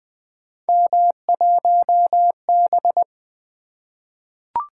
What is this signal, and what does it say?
Morse code, heard as a keyed tone, about 20 wpm, 700 Hz. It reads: M1B
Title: Three Letter-Number Combinations - 20wpm